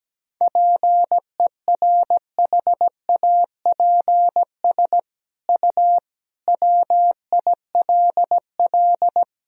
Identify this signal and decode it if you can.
Morse code, heard as a keyed tone, about 17 wpm, 700 Hz. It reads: PERHAPS U WILL
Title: Sentences from Top 500 Words and Abbreviations 17wpm